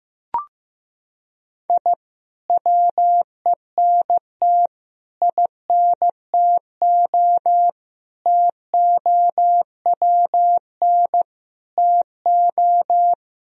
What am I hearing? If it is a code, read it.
Morse code, heard as a keyed tone, about 16 wpm, 700 Hz. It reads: I WENT INTO TOWN TO
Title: Sentences from Top 1000 Words - Review Part 4 - 15wpm